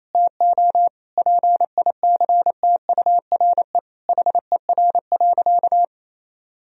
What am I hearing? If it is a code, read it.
Morse code, heard as a keyed tone, about 27 wpm, 700 Hz. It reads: TO PICTURE HER.
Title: Sentences from Top 1000 Words - Review Part 1 - 28wpm